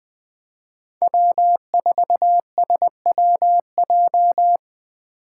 Morse code, heard as a keyed tone, about 24 wpm, 700 Hz. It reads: W4SWJ
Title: Speed Racing - USA 1x3 Call Signs 20wpm